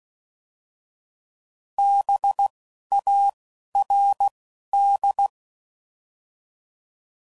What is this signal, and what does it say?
Morse code, heard as a keyed tone, about 16 wpm, 800 Hz. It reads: B A R D